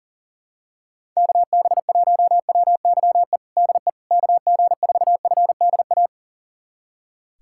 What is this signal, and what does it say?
Morse code, heard as a keyed tone, about 40 wpm, 690 Hz. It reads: KB1WYE DE KG4FDA
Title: USA Callsign DE Callsign - 6 Characters - 40wpm